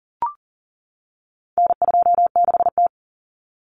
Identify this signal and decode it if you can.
Morse code, heard as a keyed tone, about 40 wpm, 700 Hz. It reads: N26T
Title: Common Cut Numbers - 4-Digit 40wpm